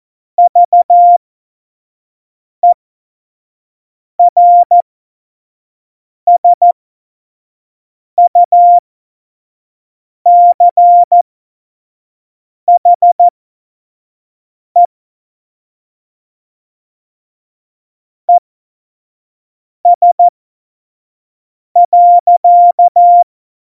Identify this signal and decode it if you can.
Morse code, heard as a keyed tone, about 14 wpm, spaced as for 5 wpm, 700 Hz. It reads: VERSUCHE ES.
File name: Morsekurs / Witze / Witz 26 Mai 2020, 25 F70